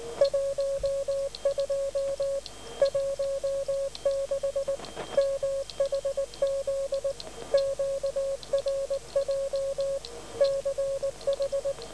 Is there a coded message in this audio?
Yes: Morse code from 0.2 to 11.7 s, heard as a keyed tone, about 19 wpm, 550 Hz. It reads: 12 16 MHZ QRJ CH